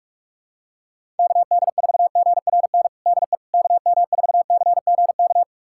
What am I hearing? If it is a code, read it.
Morse code, heard as a keyed tone, about 46 wpm, 700 Hz. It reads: KD4GRN DE KM4CGK
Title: USA Callsign DE Callsign - 6 Characters - 45wpm